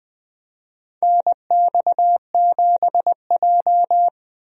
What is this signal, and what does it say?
Morse code, heard as a keyed tone, about 21 wpm, 700 Hz. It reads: NX7J
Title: USA Call Signs - 4 Characters - 20wpm